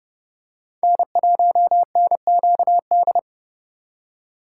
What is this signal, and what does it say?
Morse code, heard as a keyed tone, about 30 wpm, 700 Hz. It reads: N1NQD